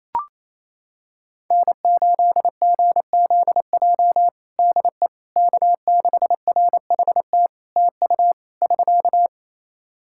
Morse code, heard as a keyed tone, about 27 wpm, 700 Hz. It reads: N8GZJ DE K6RHT TU <SK>
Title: Practice QSO - Exchange 4 - Ending - 28wpm